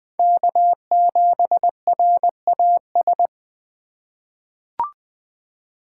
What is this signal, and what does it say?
Morse code, heard as a keyed tone, about 20 wpm, 700 Hz. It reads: K7RAS
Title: USA Call Signs - 5 Characters - 4x Character Spacing - 20wpm